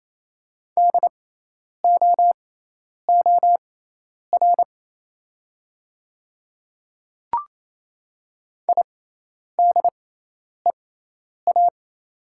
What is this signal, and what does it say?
Morse code, heard as a keyed tone, about 27 wpm, spaced as for 10 wpm, 690 Hz. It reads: DOOR IDEA
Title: 6x Character Spacing - Top 300 Words - 28wpm